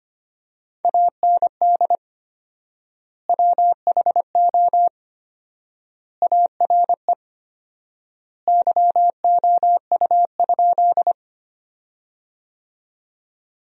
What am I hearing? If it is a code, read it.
Morse code, heard as a keyed tone, about 25 wpm, 700 Hz. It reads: AND WHO ARE YOU?
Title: 4x Word-Spacing - Sentences from Top 100 Words - 25wpm